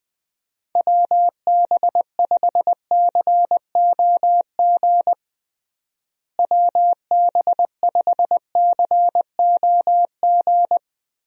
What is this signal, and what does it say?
Morse code, heard as a keyed tone, about 19 wpm, 700 Hz. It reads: WB5COG WB5COG
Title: USA Call Signs - 6 Characters - Repeated 3x - 20wpm